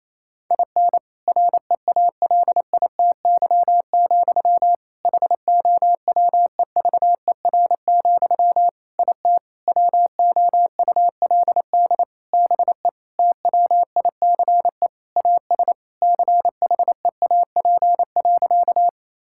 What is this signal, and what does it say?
Morse code, heard as a keyed tone, about 28 wpm, 700 Hz. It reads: IN REALITY, HOWEVER, IT WOULD BE TWICE AS CHEAP.
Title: Sentences from Top 1600 Words 28wpm